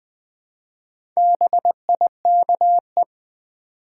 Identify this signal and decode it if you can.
Morse code, heard as a keyed tone, about 20 wpm, 700 Hz. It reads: BIKE
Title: Build A Word - 1 Letter At A Time - 4-Letter Words - 20wpm